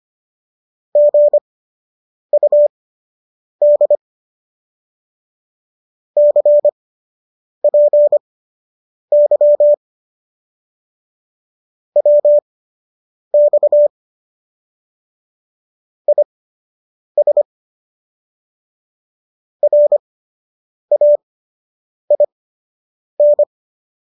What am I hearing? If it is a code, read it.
Morse code, heard as a keyed tone, about 25 wpm, spaced as for 8 wpm, 590 Hz. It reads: GUD CPY WX IS RAIN
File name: qso003 08